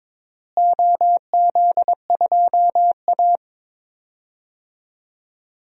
Morse code, heard as a keyed tone, about 22 wpm, 700 Hz. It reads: OZ2A
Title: DXCC-100 Call Signs - 4 Characters - Repeated 3x - 22wpm